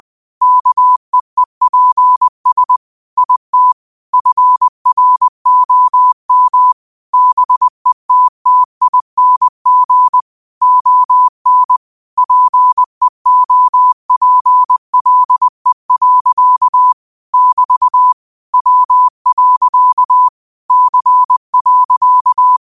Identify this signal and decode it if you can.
Morse code, heard as a keyed tone, about 20 wpm, 1000 Hz. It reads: KEEPS IT FROM BETTING ON PEOPLE. <BT> W. C.
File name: qotdfiles / 20wpm